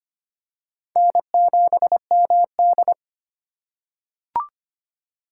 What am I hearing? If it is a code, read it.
Morse code, heard as a keyed tone, about 25 wpm, 700 Hz. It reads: N7MD
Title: USA Call Signs - 4 Characters - 3x Character Spacing - 25wpm